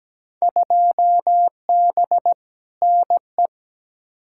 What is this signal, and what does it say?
Morse code, heard as a keyed tone, about 17 wpm, 700 Hz. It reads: 2B NE
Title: Field Day Exchanges Encore - 17wpm